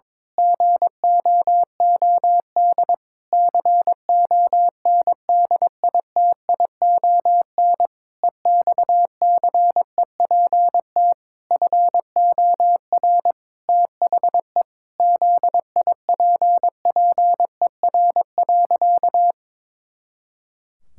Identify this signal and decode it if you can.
Morse code, heard as a keyed tone, about 22 wpm, 700 Hz. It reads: GOOD CONDITION EXCEPT FOR THE ZIPPER.
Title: Pangram Challenge v2 - 22wpm